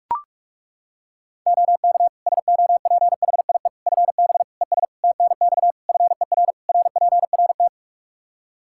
Morse code, heard as a keyed tone, about 42 wpm, 700 Hz. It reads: OK SOPHIE FB ES TNX FER RPRT
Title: Practice QSO - Exchange 2 - Info - 45wpm